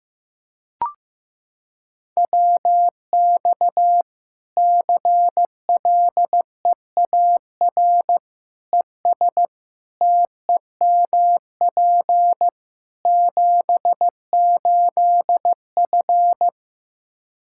Morse code, heard as a keyed tone, about 16 wpm, 700 Hz. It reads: WX CLEAR ES TEMP 78F
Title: K1USN SST QSO Practice 15wpm